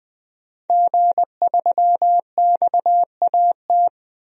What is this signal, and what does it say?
Morse code, heard as a keyed tone, about 20 wpm, 700 Hz. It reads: G3XAT
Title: DXCC-100 Call Signs - 5 Characters - 20wpm